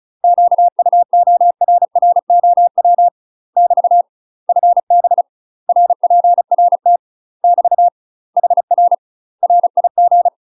Morse code, heard as a keyed tone, about 34 wpm, 700 Hz. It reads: QUORROW <BT> FB RPRT <BT> HR RIG
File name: UK MASTER QSOs TRAINING / 35WPM / UK MASTER QSOs 0036